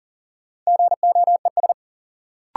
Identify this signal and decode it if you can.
Morse code, heard as a keyed tone, about 40 wpm, 700 Hz. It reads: GOES
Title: Sentences from Top 1000 Words - Review Part 5 - New Word Drill - 40wpm